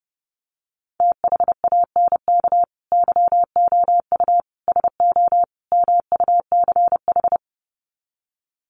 Morse code, heard as a keyed tone, about 30 wpm, 700 Hz. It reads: THANK YOU SO MUCH
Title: Sets of 4 Words 30wpm